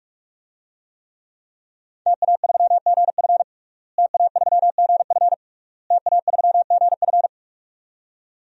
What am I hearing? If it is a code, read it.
Morse code, heard as a keyed tone, about 42 wpm, 700 Hz. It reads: TA3GF TA3GF TA3GF
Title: DXCC-100 Call Signs - 5 Characters - Repeated 3x - 45wpm